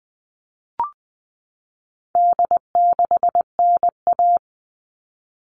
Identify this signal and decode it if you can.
Morse code, heard as a keyed tone, about 20 wpm, 700 Hz. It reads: D6NA
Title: All Cut Numbers - 4-Digit 20wpm